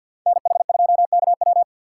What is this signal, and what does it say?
Morse code, heard as a keyed tone, about 52 wpm, 700 Hz. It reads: NH2KW
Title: USA Call Signs - 5 Characters - 2x Character Spacing - 50wpm